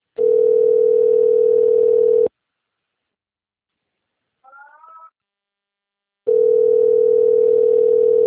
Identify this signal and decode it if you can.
Ringback tone (North America), 440 + 480 Hz.
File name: rg-1600-unknown- - - .43345